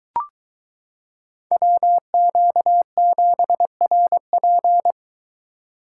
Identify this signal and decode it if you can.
Morse code, heard as a keyed tone, about 22 wpm, 700 Hz. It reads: WQ7RP